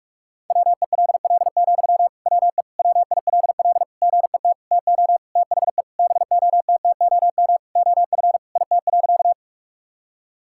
Morse code, heard as a keyed tone, about 47 wpm, 700 Hz. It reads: WELL, WE WILL GET TO THE BOTTOM OF IT.
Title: Sentences from Top 1000 Words - Review Part 6 - 45wpm